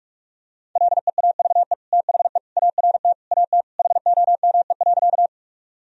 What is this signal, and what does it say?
Morse code, heard as a keyed tone, about 46 wpm, 700 Hz. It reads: LEAVE THE ART AT HOME.
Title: Sentences from Top 1000 Words - Review Part 3 - 45wpm